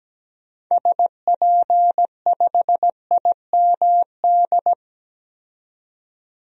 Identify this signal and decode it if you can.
Morse code, heard as a keyed tone, about 17 wpm, 700 Hz. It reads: SP5IMD
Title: DXCC-100 Call Signs - 6 Characters - 17wpm